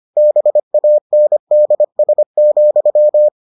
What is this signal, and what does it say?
Morse code, heard as a keyed tone, about 25 wpm, 600 Hz. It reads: BANDS,